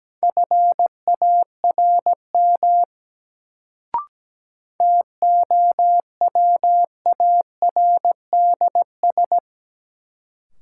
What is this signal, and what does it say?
Morse code, heard as a keyed tone, about 17 wpm, 700 Hz. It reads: FARM TOWARDS
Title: Sentences from Top 1000 Words - Review Part 2 - New Word Drill - 17wpm